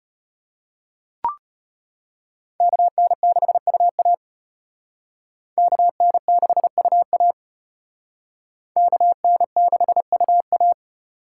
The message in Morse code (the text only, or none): KN6UA KN6UA KN6UA